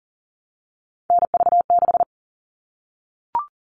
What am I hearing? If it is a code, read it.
Morse code, heard as a keyed tone, about 40 wpm, 700 Hz. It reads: NV6
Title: All Cut Numbers - 3-Digit 40wpm